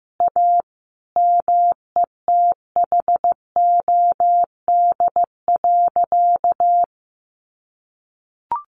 Morse code, heard as a keyed tone, about 15 wpm, 700 Hz. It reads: A METHOD.
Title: Sentences from Top 600 Words 15wpm